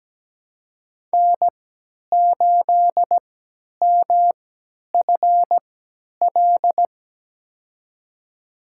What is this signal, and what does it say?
Morse code, heard as a keyed tone, about 17 wpm, spaced as for 10 wpm, 700 Hz. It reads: N8MFL
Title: USA Call Signs - 3x Character Spacing - 17wpm